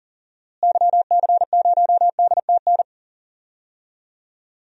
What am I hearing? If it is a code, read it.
Morse code, heard as a keyed tone, about 40 wpm, 700 Hz. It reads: YC0DTN
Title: DXCC-100 Call Signs - 6 Characters - Repeated 2x - 40wpm